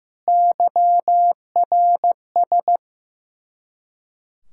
Morse code, heard as a keyed tone, about 15 wpm, 700 Hz. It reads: YRS
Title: Practice QSO - Exchange 3 - More Info - 15wpm